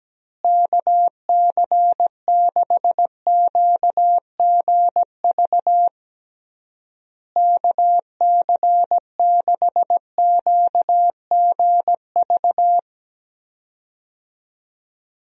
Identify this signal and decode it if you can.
Morse code, heard as a keyed tone, about 17 wpm, 700 Hz. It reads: KC6QGV KC6QGV
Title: USA Call Signs - 6 Characters - Repeated 3x - 17wpm